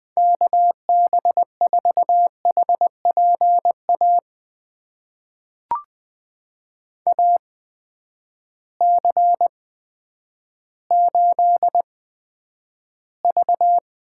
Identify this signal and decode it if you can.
Morse code, heard as a keyed tone, about 20 wpm, spaced as for 10 wpm, 700 Hz. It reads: KB4HPA AC8V
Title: USA Call Signs - 8x Character Spacing - 20wpm